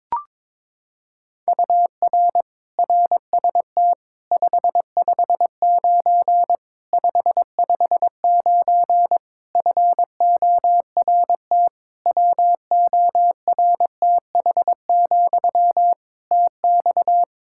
Morse code, heard as a keyed tone, about 22 wpm, 700 Hz. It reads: UR RST 559 559 FORT WORTH, TX
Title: Practice QSO - Exchange 1 - Beginning - 22wpm